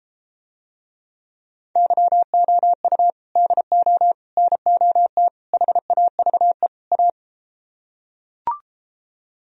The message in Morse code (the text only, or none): YOU DO NOT HAVE A